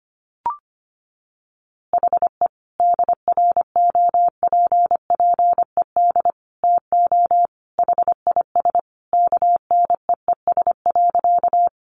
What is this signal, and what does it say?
Morse code, heard as a keyed tone, about 25 wpm, 700 Hz. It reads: HE DROPPED TO HIS KNEES.
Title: A Sound Of Thunder 25wpm